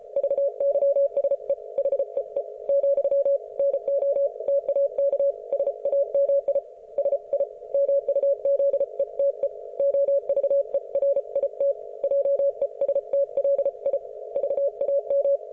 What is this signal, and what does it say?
Morse code, heard as a keyed tone, about 34 wpm, 560 Hz. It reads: VYSE HEE , NO TAK SAMI SI MUZETE OVERIT JESTLI VAM